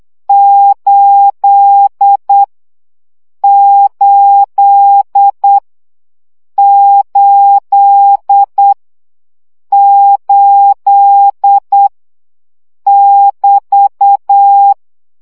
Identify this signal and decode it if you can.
Morse code, heard as a keyed tone, about 8 wpm, 800 Hz. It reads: 8 8 8 8 <BT>
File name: CW gyakorlas 8 7 baud